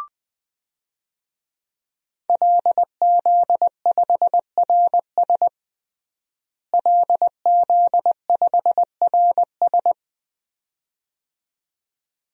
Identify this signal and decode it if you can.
Morse code, heard as a keyed tone, about 20 wpm, 700 Hz. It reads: LZ5RS LZ5RS
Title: DXCC-100 Call Signs - 5 Characters - Repeated 2x - 20wpm